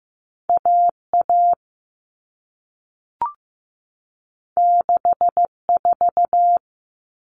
Morse code, heard as a keyed tone, about 15 wpm, 700 Hz. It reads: AA 64
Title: All Cut Numbers - 2-Digit 15wpm